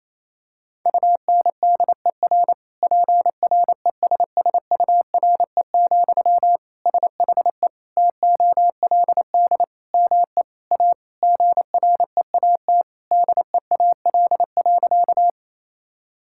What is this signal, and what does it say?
Morse code, heard as a keyed tone, about 28 wpm, 700 Hz. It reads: UNDER PRESSURE, SHE TOLD ME A GREAT DEAL.
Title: Sentences from Top 1000 Words - Review Part 6 - 28wpm